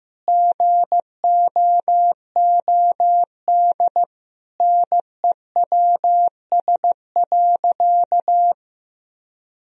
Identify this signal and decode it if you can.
Morse code, heard as a keyed tone, about 15 wpm, 700 Hz. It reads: GOOD NEWS.
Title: Sentences from Top 1000 Words - Review Part 1 - 15wpm